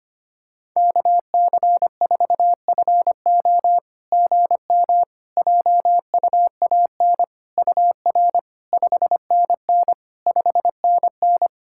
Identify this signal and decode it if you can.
Morse code, heard as a keyed tone, about 25 wpm, 700 Hz. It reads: KC4FO GM JUAN UR 5NN 5NN
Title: SOTA QSO Practice 25wpm